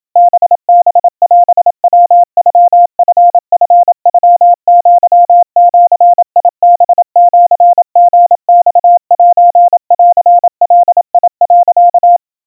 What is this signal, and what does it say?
Morse code (Japanese Wabun code), heard as a keyed tone, about 27 wpm, 700 Hz. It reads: ハハオヤノチチノアジハシリマセンガ、
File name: kawa / Morse sample / kamyu 100 cpm 700 Hz Crispy